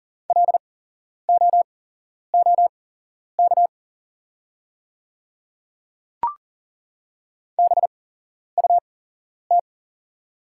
Morse code, heard as a keyed tone, about 39 wpm, spaced as for 11 wpm, 700 Hz. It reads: LOOK BUT
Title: 8x Character Spacing - Top 100 Words - 40wpm